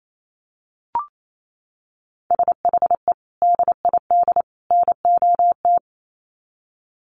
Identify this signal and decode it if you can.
Morse code, heard as a keyed tone, about 28 wpm, 700 Hz. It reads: SHE DID NOT
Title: Sets of 3 Words 28wpm